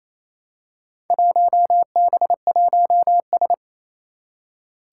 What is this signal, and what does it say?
Morse code, heard as a keyed tone, about 28 wpm, 700 Hz. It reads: 1B1S
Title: Four Challenging Letter-Number Combinations 28wpm